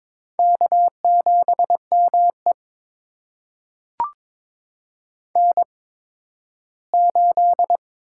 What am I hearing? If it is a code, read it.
Morse code, heard as a keyed tone, about 22 wpm, spaced as for 13 wpm, 700 Hz. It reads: K7ME N8
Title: USA Call Signs - 8x Character Spacing - 22wpm